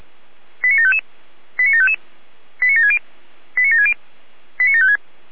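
Five-tone selective call (ZVEI), heard as quick runs of five tones, tones of about 70 ms. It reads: 89755 89755 89760 89760 89756
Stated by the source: ZVEI five-tone call 89715 received on SDR.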